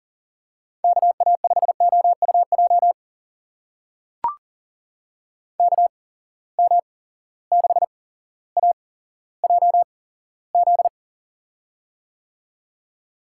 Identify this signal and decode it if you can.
Morse code, heard as a keyed tone, about 40 wpm, spaced as for 17 wpm, 700 Hz. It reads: KA5OUJ KM6AJZ
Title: USA Call Signs - 6 Characters - 8x Character Spacing - 40wpm